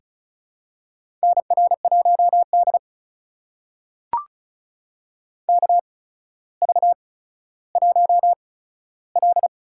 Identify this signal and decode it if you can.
Morse code, heard as a keyed tone, about 36 wpm, spaced as for 15 wpm, 700 Hz. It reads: NR1D KV1L
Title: USA Call Signs - 4 Characters - 8x Character Spacing - 35wpm